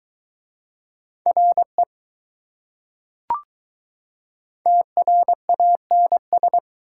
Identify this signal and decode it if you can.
Morse code, heard as a keyed tone, about 21 wpm, 700 Hz. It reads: RE TRANS